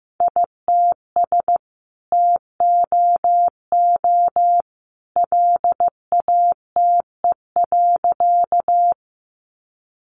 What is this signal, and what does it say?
Morse code, heard as a keyed tone, about 15 wpm, 700 Hz. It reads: ITS TOO LATE.
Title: Sentences from Top 600 Words 15wpm